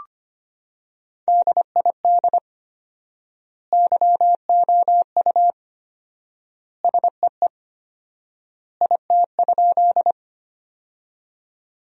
Morse code, heard as a keyed tone, about 25 wpm, 700 Hz. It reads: DID YOU SEE IT?
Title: 4x Word-Spacing - Sentences from Top 100 Words - 25wpm